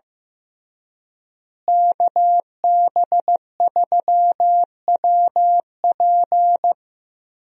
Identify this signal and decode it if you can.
Morse code, heard as a keyed tone, about 15 wpm, 700 Hz. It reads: KB3WP
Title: USA Call Signs - 5 Characters - Repeated 2x - 15wpm